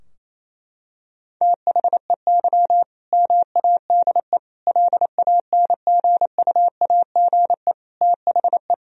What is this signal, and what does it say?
Morse code, heard as a keyed tone, about 27 wpm, 700 Hz. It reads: THEY MADE LANGUAGE THE
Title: Sentences from Top 1000 Words - Review Part 3 - 28wpm